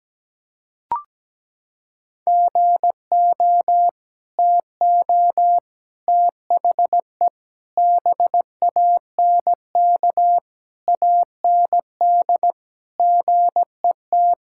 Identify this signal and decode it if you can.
Morse code, heard as a keyed tone, about 17 wpm, 700 Hz. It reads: GO TO THE BANK AND GET
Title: Sentences from Top 1000 Words - Review Part 5 - 17wpm